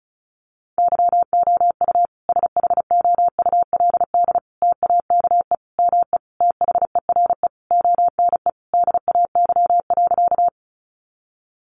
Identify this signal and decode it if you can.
Morse code, heard as a keyed tone, about 34 wpm, 700 Hz. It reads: YOU SHOULD TAKE ME THERE ONE DAY.
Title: Sentences from Top 300 Words 35wpm